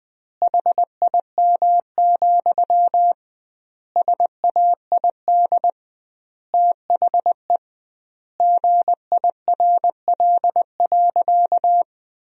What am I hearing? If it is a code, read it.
Morse code, heard as a keyed tone, about 21 wpm, 700 Hz. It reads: HIM, SAID THE GIRL.
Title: 2x Word-Spacing - Sentences from Top 400 Words - 20wpm